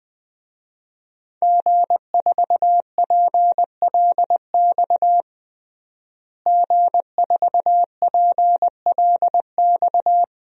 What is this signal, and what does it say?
Morse code, heard as a keyed tone, about 20 wpm, 700 Hz. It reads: G4PLX G4PLX
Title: DXCC-100 Call Signs - 5 Characters - Repeated 3x - 20wpm